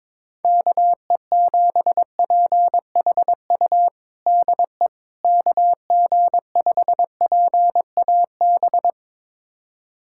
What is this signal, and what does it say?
Morse code, heard as a keyed tone, about 21 wpm, 700 Hz. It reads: KE7PHU DE KG5PAB
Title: USA Callsign DE Callsign - 6 Characters - 22wpm